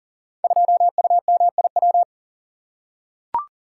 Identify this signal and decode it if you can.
Morse code, heard as a keyed tone, about 38 wpm, 700 Hz. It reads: 2UMIW
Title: Code Groups - 5 Letter:Numbers - 40wpm